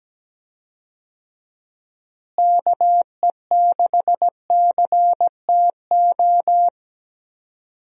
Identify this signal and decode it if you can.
Morse code, heard as a keyed tone, about 17 wpm, 700 Hz. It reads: KE6CTO
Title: USA Call Signs - 6 Characters - Repeated 2x - 17wpm